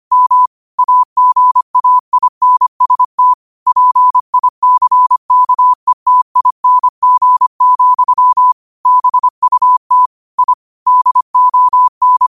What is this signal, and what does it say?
Morse code, heard as a keyed tone, about 25 wpm, 1000 Hz. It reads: M AGAINST PICKETING, BUT I DON